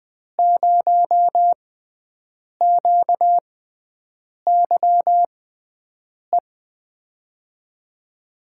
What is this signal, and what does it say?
Morse code, heard as a keyed tone, about 20 wpm, spaced as for 7 wpm, 700 Hz. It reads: 0QYE
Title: USA Call Signs - 6 Characters - 6x Character Spacing - 20wpm